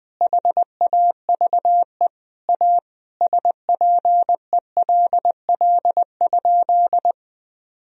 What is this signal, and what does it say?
Morse code, heard as a keyed tone, about 20 wpm, 700 Hz. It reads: HAVE A SPELL?
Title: Sentences from Top 1000 Words - Review Part 1 - 20wpm